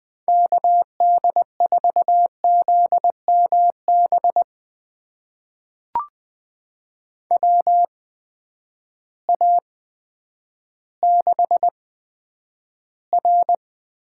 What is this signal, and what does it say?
Morse code, heard as a keyed tone, about 20 wpm, spaced as for 10 wpm, 700 Hz. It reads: KD4ZMB WA6R
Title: USA Call Signs - 6 Characters - 8x Character Spacing - 20wpm